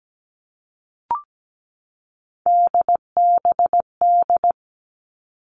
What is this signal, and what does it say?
Morse code, heard as a keyed tone, about 17 wpm, 700 Hz. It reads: DBD
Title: All Cut Numbers - 3-Digit 17wpm